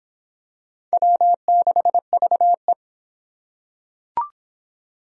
Morse code, heard as a keyed tone, about 24 wpm, 700 Hz. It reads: W6VE